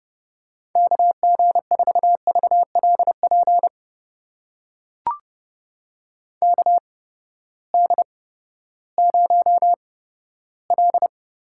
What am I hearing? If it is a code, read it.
Morse code, heard as a keyed tone, about 30 wpm, spaced as for 15 wpm, 700 Hz. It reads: KG4VLP KD0L
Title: USA Call Signs - 6 Characters - 8x Character Spacing - 30wpm